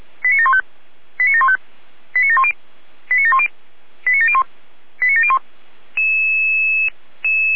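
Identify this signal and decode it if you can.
Five-tone selective call (ZVEI), heard as quick runs of five tones, tones of about 70 ms. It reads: 89715 89715 89710 89710 89701 89701
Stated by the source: ZVEI five-tone selective call 89490 received on SDR.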